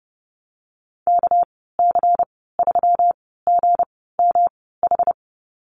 Morse code, heard as a keyed tone, about 30 wpm, spaced as for 17 wpm, 700 Hz. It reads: KC3GMH
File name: Lesson 37g - Letter x - Call Signs 30wpm